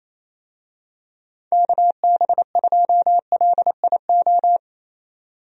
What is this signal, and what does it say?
Morse code, heard as a keyed tone, about 28 wpm, 700 Hz. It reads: KB2LIO
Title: USA Call Signs - 6 Characters - 28wpm